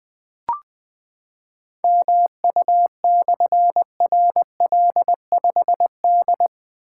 Morse code, heard as a keyed tone, about 20 wpm, 700 Hz. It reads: MU/RL5D
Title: Difficult Call Signs 20wpm